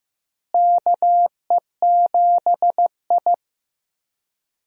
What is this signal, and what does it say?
Morse code, heard as a keyed tone, about 15 wpm, 700 Hz. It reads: KE7I
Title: USA Call Signs - 4 Characters - Repeated 2x - 15wpm